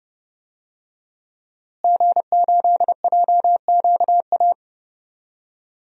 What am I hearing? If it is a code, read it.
Morse code, heard as a keyed tone, about 31 wpm, 700 Hz. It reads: G8JQA
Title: DXCC-100 Call Signs - 5 Characters - Repeated 3x - 30wpm